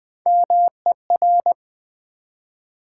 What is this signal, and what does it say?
Morse code, heard as a keyed tone, about 20 wpm, 700 Hz. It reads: MER
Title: Build A Word - 1 Letter At A Time - 5-Letter Words - 20wpm